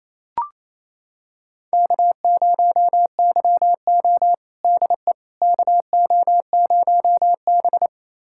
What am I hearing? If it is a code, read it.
Morse code, heard as a keyed tone, about 28 wpm, 700 Hz. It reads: K0YO DE KO0B
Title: USA Callsign DE Callsign - 4 Characters - 28wpm